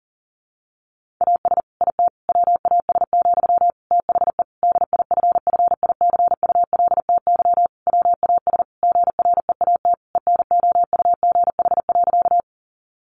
Morse code, heard as a keyed tone, about 37 wpm, 700 Hz. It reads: AS IT WAS, THE DIFFICULTY WAS GREAT ENOUGH.
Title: Sentences from Top 1500 Words 40wpm